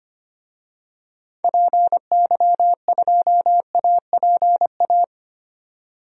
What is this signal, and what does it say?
Morse code, heard as a keyed tone, about 25 wpm, 700 Hz. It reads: PY2APA
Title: DXCC-100 Call Signs - 6 Characters - 25wpm